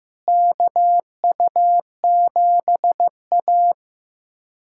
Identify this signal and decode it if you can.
Morse code, heard as a keyed tone, about 15 wpm, 700 Hz. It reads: KU7A
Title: USA Callsign DE Callsign - 4 Characters - 15wpm